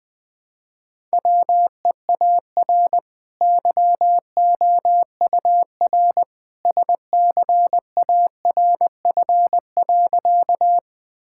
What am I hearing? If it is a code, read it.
Morse code, heard as a keyed tone, about 20 wpm, 700 Hz. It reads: WEAR YOUR SCARF.
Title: Easy Sentences with Hints - 20wpm